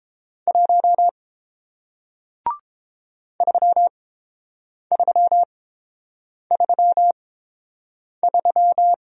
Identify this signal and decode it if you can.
Morse code, heard as a keyed tone, about 30 wpm, spaced as for 7 wpm, 700 Hz. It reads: 1 3333